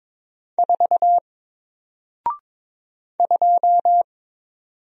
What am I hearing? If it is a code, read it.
Morse code, heard as a keyed tone, about 22 wpm, 700 Hz. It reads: 4 2
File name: Single-Letter-Number-22wpm